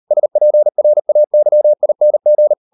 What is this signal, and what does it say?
Morse code, heard as a keyed tone, about 38 wpm, 600 Hz. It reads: SPRAYING